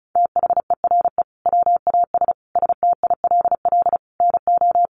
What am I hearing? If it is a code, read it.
Morse code, heard as a keyed tone, about 34 wpm, 700 Hz. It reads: THERE WAS STILL NO
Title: Sentences from Top 1400 Words 35wpm